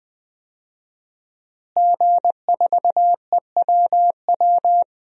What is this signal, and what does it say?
Morse code, heard as a keyed tone, about 20 wpm, 700 Hz. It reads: G4EWW
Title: DXCC-100 Call Signs - 5 Characters - Repeated 2x - 20wpm